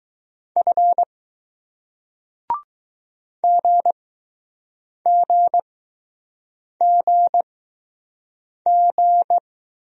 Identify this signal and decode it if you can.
Morse code, heard as a keyed tone, about 20 wpm, spaced as for 6 wpm, 700 Hz. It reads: F GGGG